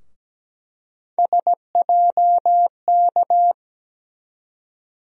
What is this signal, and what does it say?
Morse code, heard as a keyed tone, about 17 wpm, 700 Hz. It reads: SJK
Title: USA Call Sign Postfixes - 3 Letters 17wpm